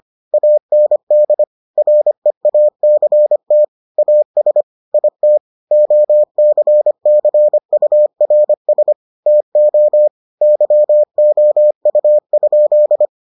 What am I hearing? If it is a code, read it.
Morse code, heard as a keyed tone, about 25 wpm, 600 Hz. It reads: AND REACT AS IT OCCURS TO YOU?